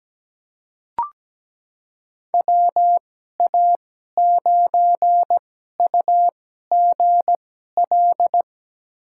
Morse code, heard as a keyed tone, about 17 wpm, spaced as for 12 wpm, 700 Hz. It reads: WA9UGL
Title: USA Call Signs - 6 Characters - 2x Character Spacing - 17wpm